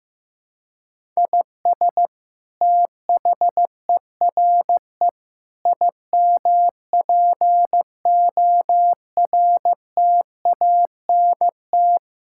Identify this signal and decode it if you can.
Morse code, heard as a keyed tone, about 15 wpm, 700 Hz. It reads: IS THERE IMPORTANT
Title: Sentences from Top 1000 Words - Review Part 6 - 15wpm